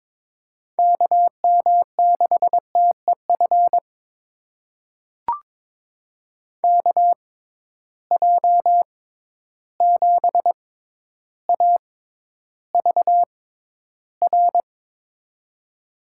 Morse code, heard as a keyed tone, about 22 wpm, spaced as for 11 wpm, 700 Hz. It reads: KM6TEF KJ7AVR
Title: USA Call Signs - 6 Characters - 6x Character Spacing - 22wpm